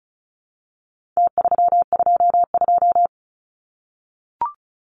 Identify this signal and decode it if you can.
Morse code, heard as a keyed tone, about 34 wpm, 700 Hz. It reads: T322
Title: Numbers - Contest Style - 4-Digit 35wpm